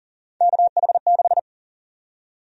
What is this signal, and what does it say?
Morse code, heard as a keyed tone, about 40 wpm, 700 Hz. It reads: KH6
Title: USA Call Sign Prefixes - 40wpm